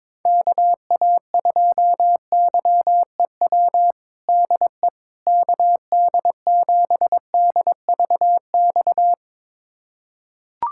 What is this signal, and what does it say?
Morse code, heard as a keyed tone, about 22 wpm, 700 Hz. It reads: KA2YEW DE KD7DVX
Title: USA Callsign DE Callsign - 6 Characters - 22wpm